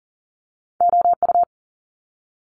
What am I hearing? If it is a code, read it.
Morse code, heard as a keyed tone, about 41 wpm, 700 Hz. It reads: OU
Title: Multi-letter Phonemes Quick Fire 40wpm